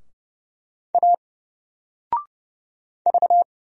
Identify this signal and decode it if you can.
Morse code, heard as a keyed tone, about 31 wpm, 700 Hz. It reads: A V
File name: Single-Letter-Number-30wpm